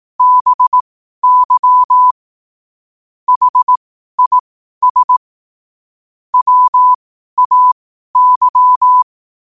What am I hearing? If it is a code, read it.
Morse code, heard as a keyed tone, about 18 wpm, spaced as for 12 wpm, 1000 Hz. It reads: BY HIS WAY
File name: qotdfiles / 12wpm